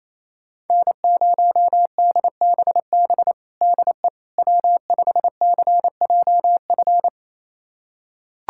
Morse code, heard as a keyed tone, about 28 wpm, 700 Hz. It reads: N0DBB DE W5CJF
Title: USA Callsign DE Callsign - 5 Characters - 28wpm